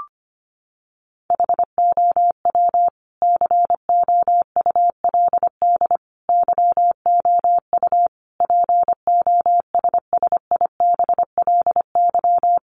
Morse code, heard as a keyed tone, about 25 wpm, 700 Hz. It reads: HOW COULD YOU POSSIBLY
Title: Sentences from Top 1300 Words 25wpm